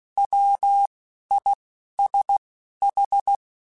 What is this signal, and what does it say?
Morse code, heard as a keyed tone, about 16 wpm, 800 Hz. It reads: W I S H
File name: Alphabetical / Alphabetical mp3s / Ws @10